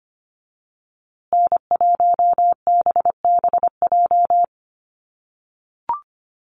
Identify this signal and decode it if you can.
Morse code, heard as a keyed tone, about 25 wpm, 700 Hz. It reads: N1BBJ
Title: USA Call Signs Repeated 3x 25wpm